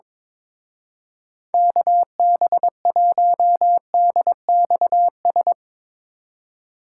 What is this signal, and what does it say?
Morse code, heard as a keyed tone, about 22 wpm, 700 Hz. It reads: KB1DXS
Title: Speed Racing - USA 2x3 Call Signs 22wpm